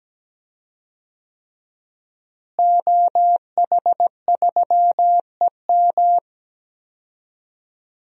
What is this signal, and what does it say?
Morse code, heard as a keyed tone, about 17 wpm, 700 Hz. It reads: OH3EM
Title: DXCC-100 Call Signs - 5 Characters - Repeated 3x - 17wpm